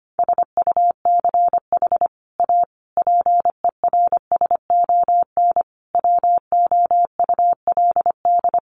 Morse code, heard as a keyed tone, about 25 wpm, 700 Hz. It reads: SUCH A PERSON WOULD
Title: Sentences from Top 1400 Words 25wpm